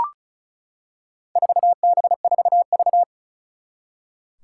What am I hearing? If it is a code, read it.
Morse code, heard as a keyed tone, about 36 wpm, 700 Hz. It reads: 4B4V
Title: Four Challenging Letter-Number Combinations 35wpm